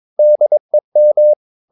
Morse code, heard as a keyed tone, about 22 wpm, 600 Hz. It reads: DEM